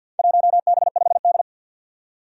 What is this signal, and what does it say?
Morse code, heard as a keyed tone, about 48 wpm, 700 Hz. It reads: 1B5D
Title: Four Challenging Letter-Number Combinations 50wpm